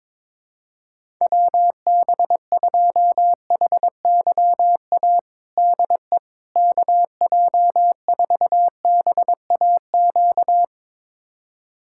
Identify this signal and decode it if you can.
Morse code, heard as a keyed tone, about 21 wpm, 700 Hz. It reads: WB2HYA DE KJ4BAQ
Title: USA Callsign DE Callsign - 6 Characters - 22wpm